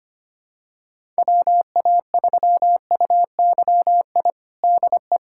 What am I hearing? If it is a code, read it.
Morse code, heard as a keyed tone, about 25 wpm, 700 Hz. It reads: WA3UYI DE
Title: USA Callsign DE Callsign - 6 Characters - 25wpm